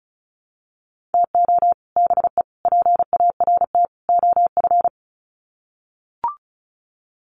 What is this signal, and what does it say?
Morse code, heard as a keyed tone, about 34 wpm, 700 Hz. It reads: TO BE PART OF
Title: Sets of 4 Words 35wpm